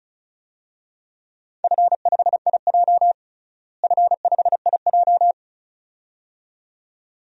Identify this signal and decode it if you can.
Morse code, heard as a keyed tone, about 36 wpm, 700 Hz. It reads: F5IJ F5IJ
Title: DXCC-100 Call Signs - 4 Characters - Repeated 2x - 35wpm